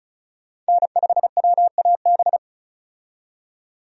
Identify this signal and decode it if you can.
Morse code, heard as a keyed tone, about 36 wpm, 700 Hz. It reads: N5WAB
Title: USA Call Signs - 5 Characters - 35wpm